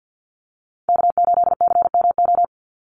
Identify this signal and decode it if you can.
Morse code, heard as a keyed tone, about 50 wpm, 700 Hz. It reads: K8CMO
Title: Call Signs Repeated 3x - 50wpm